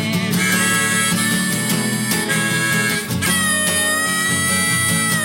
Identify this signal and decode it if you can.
Emergency Alert System (SAME) tones (header bursts), heard as harsh buzzing bursts. It is heard from 0.5 to 4.9 s.